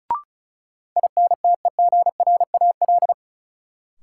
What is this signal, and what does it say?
Morse code, heard as a keyed tone, about 34 wpm, 700 Hz. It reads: INTEGRAL
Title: Sets of 5 Words - New Word Drill - 35wpm